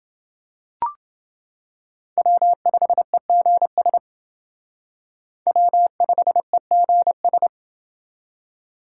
Morse code, heard as a keyed tone, about 27 wpm, 700 Hz. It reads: W5EGS W5EGS